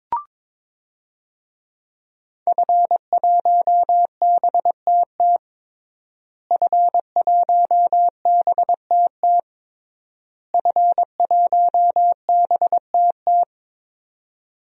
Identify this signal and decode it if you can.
Morse code, heard as a keyed tone, about 22 wpm, 700 Hz. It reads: F1BTT F1BTT F1BTT
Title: DXCC-100 Call Signs - 5 Characters - Repeated 3x - 22wpm